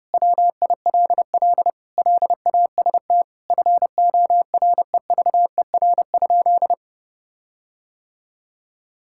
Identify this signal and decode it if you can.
Morse code, heard as a keyed tone, about 30 wpm, 700 Hz. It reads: WILL LAST FOREVER?
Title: Sentences from Top 1600 Words 30wpm